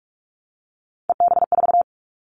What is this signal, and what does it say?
Morse code, heard as a keyed tone, about 44 wpm, 700 Hz. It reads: EB4
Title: Callsign Prefixes for Top 25 DXCC - Repeated 3x - 45wpm